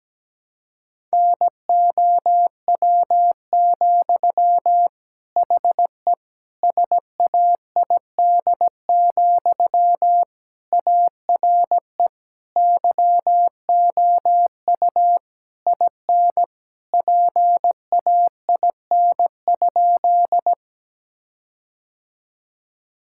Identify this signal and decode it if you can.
Morse code, heard as a keyed tone, about 17 wpm, 700 Hz. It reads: NOW, HE SAID, ARE YOU IN PAIN?
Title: Sentences from Top 1600 Words 17wpm